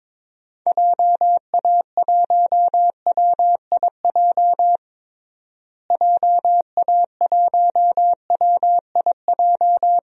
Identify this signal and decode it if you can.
Morse code, heard as a keyed tone, about 22 wpm, 700 Hz. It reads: JA1WIJ JA1WIJ
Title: DXCC-100 Call Signs - 6 Characters - Repeated 2x - 22wpm